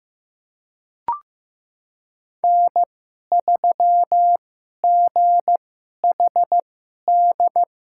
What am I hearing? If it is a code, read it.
Morse code, heard as a keyed tone, about 16 wpm, spaced as for 11 wpm, 700 Hz. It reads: N3GHD
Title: USA Call Signs - 2x Character Spacing - 15wpm